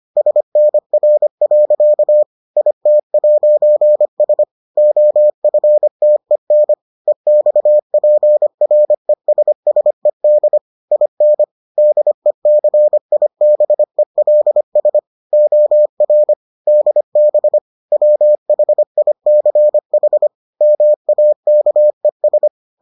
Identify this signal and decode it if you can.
Morse code, heard as a keyed tone, about 25 wpm, 600 Hz. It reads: SNR. IT'S OFTEN EXPRESSED IN DECIBELS OR DB WHICH MAKES